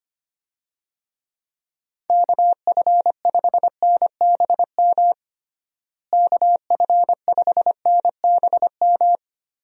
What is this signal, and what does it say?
Morse code, heard as a keyed tone, about 25 wpm, 700 Hz. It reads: KF5NBM KF5NBM
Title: USA Call Signs - 6 Characters - Repeated 2x - 25wpm